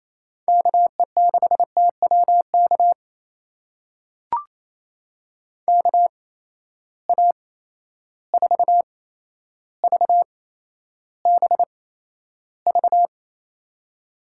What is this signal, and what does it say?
Morse code, heard as a keyed tone, about 28 wpm, spaced as for 12 wpm, 700 Hz. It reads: KE6TWK KA4VBV
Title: USA Call Signs - 8x Character Spacing - 28wpm